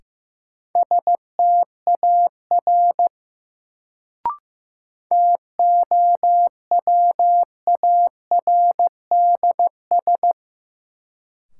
Morse code, heard as a keyed tone, about 15 wpm, 700 Hz. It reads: STAR TOWARDS
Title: Sentences from Top 1000 Words - Review Part 2 - New Word Drill - 15wpm